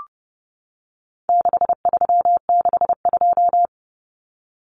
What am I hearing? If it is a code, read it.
Morse code, heard as a keyed tone, about 30 wpm, 700 Hz. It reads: 6362
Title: Numbers - 4-Digit 30wpm